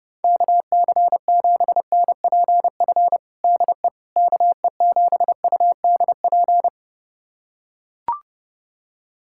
Morse code, heard as a keyed tone, about 30 wpm, 700 Hz. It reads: KC7NPF DE KE7UDP
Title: USA Callsign DE Callsign - 6 Characters - 30wpm